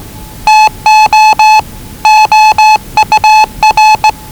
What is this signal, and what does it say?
Morse code, heard as a keyed tone, about 20 wpm, 840 Hz. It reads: TO OUR
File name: WellingtonRadioZlwCloseDownOnHfAndMf / 01-Zlw 4277khz 417.5 qrt until 0925z